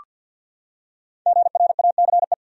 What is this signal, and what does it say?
Morse code, heard as a keyed tone, about 48 wpm, 700 Hz. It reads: GRACE
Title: USA Common Names - 50wpm